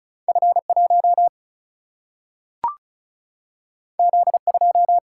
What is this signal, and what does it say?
Morse code, heard as a keyed tone, about 36 wpm, 700 Hz. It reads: F1 Z2
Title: Single-Letter Single-Number 35wpm